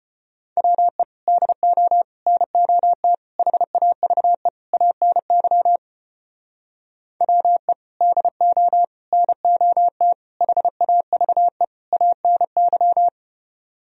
Morse code, heard as a keyed tone, about 32 wpm, 700 Hz. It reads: WE DO NOT HAVE ANY WE DO NOT HAVE ANY